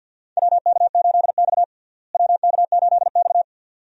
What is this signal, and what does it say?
Morse code, heard as a keyed tone, about 51 wpm, 700 Hz. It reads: WK8X WK8X
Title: USA Call Signs - 4 Characters - Repeated 3x - 50wpm